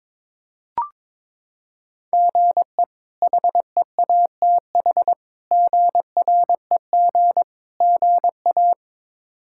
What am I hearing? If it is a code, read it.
Morse code, heard as a keyed tone, about 21 wpm, 700 Hz. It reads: GE HEATH GREG GA
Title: K1USN SST QSO Practice - 22wpm